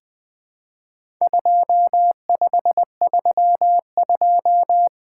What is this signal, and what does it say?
Morse code, heard as a keyed tone, about 20 wpm, 700 Hz. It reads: 2532
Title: Four Letter-Number Combinations - 20wpm